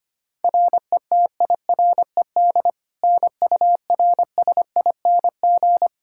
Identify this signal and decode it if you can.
Morse code, heard as a keyed tone, about 25 wpm, 700 Hz. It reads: RETIRED NURSING
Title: Practice QSO - Exchange 3 - More Info - 25wpm